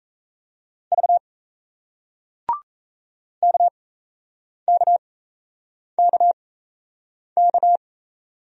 Morse code, heard as a keyed tone, about 37 wpm, spaced as for 8 wpm, 700 Hz. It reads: V KKKK